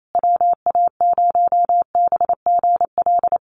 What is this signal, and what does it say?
Morse code, heard as a keyed tone, about 28 wpm, 700 Hz. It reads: WA0BGL
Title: USA Call Signs Repeated 3x 28wpm